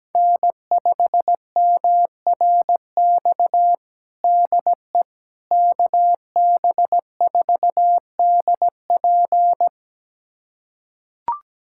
Morse code, heard as a keyed tone, about 17 wpm, 700 Hz. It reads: N5MRX DE KB4DP
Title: USA Callsign DE Callsign - 5 Characters - 17wpm